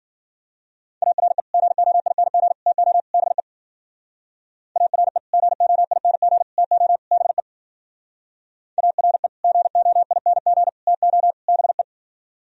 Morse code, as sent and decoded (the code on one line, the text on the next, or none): .- .-. / --. --- .. -. --. / - --- / -... / .- .-. / --. --- .. -. --. / - --- / -... / .- .-. / --. --- .. -. --. / - --- / -... .
ARE GOING TO BE ARE GOING TO BE ARE GOING TO BE